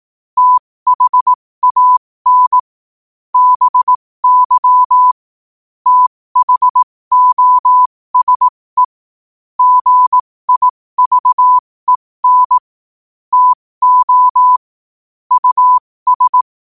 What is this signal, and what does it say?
Morse code, heard as a keyed tone, about 18 wpm, spaced as for 15 wpm, 1000 Hz. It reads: THAN BY THOSE GIVEN TO US
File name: qotdfiles / 15wpm